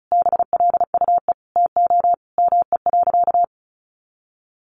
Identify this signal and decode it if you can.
Morse code, heard as a keyed tone, about 34 wpm, 700 Hz. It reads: BLUE TO ME.
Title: Sentences from Top 1200 Words 35wpm